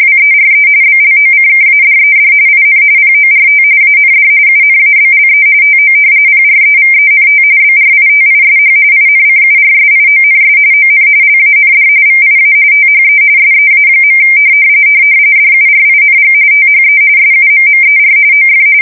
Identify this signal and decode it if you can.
RTTY, about 45.45 baud. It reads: WERE BLOCKADED IN PORT AND UNABLE TO PREVENT BRITISH INCURSIONS ON LAND VIA THE SEA. DURING THE SUMMER OF
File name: fsk170 / usnavy2